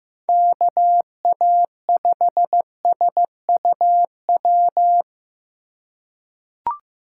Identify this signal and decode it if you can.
Morse code, heard as a keyed tone, about 15 wpm, 700 Hz. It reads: KA5SUW
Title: USA Call Signs - 3x Character Spacing - 15wpm